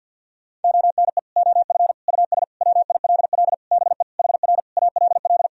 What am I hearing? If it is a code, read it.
Morse code, heard as a keyed tone, about 48 wpm, 700 Hz. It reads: ONE OF US WILL BE HR ALL
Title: Sentences from Top 300 Words and Abbreviations 50wpm